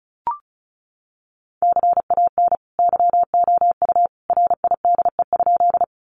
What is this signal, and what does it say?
Morse code, heard as a keyed tone, about 34 wpm, 700 Hz. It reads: CAN YOU RIDE?
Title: Sentences from Top 1200 Words 35wpm